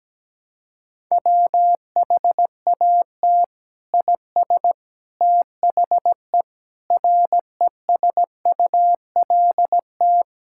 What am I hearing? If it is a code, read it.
Morse code, heard as a keyed tone, about 17 wpm, 700 Hz. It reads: WHAT IS THE RESULT
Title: Sentences from Top 1000 Words - Review Part 5 - 17wpm